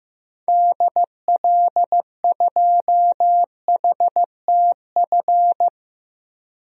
Morse code, heard as a keyed tone, about 16 wpm, 700 Hz. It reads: DL2HTF
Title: DXCC-100 Call Signs - 6 Characters - 15wpm